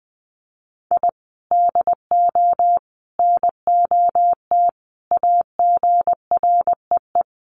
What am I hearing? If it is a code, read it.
Morse code, heard as a keyed tone, about 20 wpm, 700 Hz. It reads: I DO NOT AGREE
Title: Sets of 5 Words Encore 20wpm